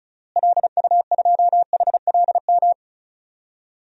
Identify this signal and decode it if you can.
Morse code, heard as a keyed tone, about 34 wpm, 700 Hz. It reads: LU2HLM
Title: DXCC-100 Call Signs - 6 Characters - Repeated 3x - 35wpm